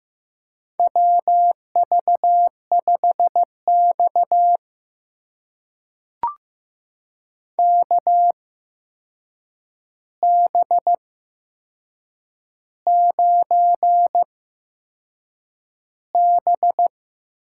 Morse code, heard as a keyed tone, about 15 wpm, spaced as for 6 wpm, 700 Hz. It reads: WV5X KB9B
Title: USA Call Signs - 8x Character Spacing - 15wpm